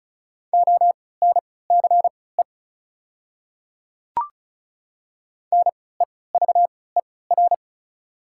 Morse code, heard as a keyed tone, about 34 wpm, spaced as for 20 wpm, 690 Hz. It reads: ONCE NEVER
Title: 3x Character Spacing - Top 300 Words - 35wpm